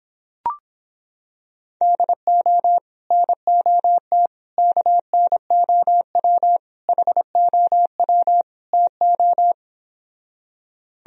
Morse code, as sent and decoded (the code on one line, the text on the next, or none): -.. --- / -. --- - / -.- -. --- .-- / .... --- .-- / - ---
DO NOT KNOW HOW TO